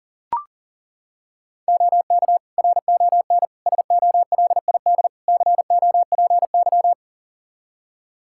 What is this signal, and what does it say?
Morse code, heard as a keyed tone, about 40 wpm, 700 Hz. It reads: OK RON SOLID COPY
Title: Practice QSO - Exchange 3 - More Info - 40wpm